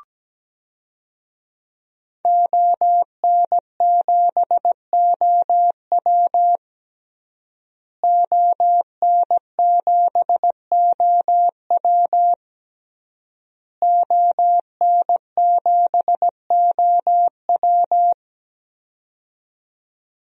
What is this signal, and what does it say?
Morse code, heard as a keyed tone, about 17 wpm, 700 Hz. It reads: ON7OW ON7OW ON7OW
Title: DXCC-100 Call Signs - 5 Characters - Repeated 3x - 17wpm